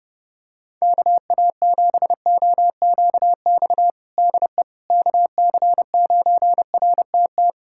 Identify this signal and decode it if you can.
Morse code, heard as a keyed tone, about 30 wpm, 700 Hz. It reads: KA7OQX DE KC9RTT
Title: USA Callsign DE Callsign - 6 Characters - 30wpm